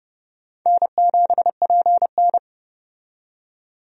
Morse code, heard as a keyed tone, about 32 wpm, 700 Hz. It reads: N7PN
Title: USA Call Signs - 4 Characters - 30wpm